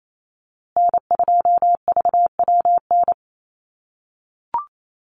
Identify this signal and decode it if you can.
Morse code, heard as a keyed tone, about 28 wpm, 700 Hz. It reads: N2VWN
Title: USA Call Signs Repeated 3x 28wpm